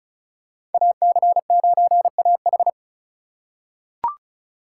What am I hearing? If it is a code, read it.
Morse code, heard as a keyed tone, about 34 wpm, 700 Hz. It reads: AC9AH
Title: USA Call Signs - 2x Character Spacing - 35wpm